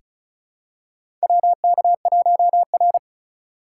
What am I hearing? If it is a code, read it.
Morse code, heard as a keyed tone, about 36 wpm, 700 Hz. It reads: WK1R
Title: USA Call Signs - 4 Characters - 3x Character Spacing - 35wpm